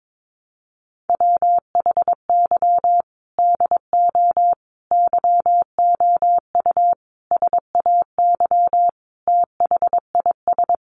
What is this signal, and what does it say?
Morse code, heard as a keyed tone, about 22 wpm, 700 Hz. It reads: WHY DO YOU SAY THIS
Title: Sentences from Top 1100 Words 22wpm